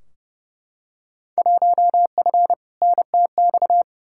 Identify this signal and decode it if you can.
Morse code, heard as a keyed tone, about 31 wpm, 700 Hz. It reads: 1F NTX
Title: Field Day Exchanges Encore - 30wpm